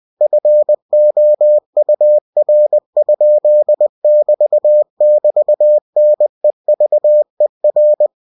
Morse code, heard as a keyed tone, about 20 wpm, 600 Hz. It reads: FOUR?<BT><BT>NEVER
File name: cw-books / winnie-the-pooh / winnie-20wpm-page-0090